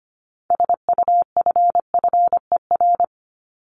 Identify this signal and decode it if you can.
Morse code, heard as a keyed tone, about 25 wpm, 700 Hz. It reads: SUFFER
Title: ING Suffix - Top 100 Words with Base Words 25wpm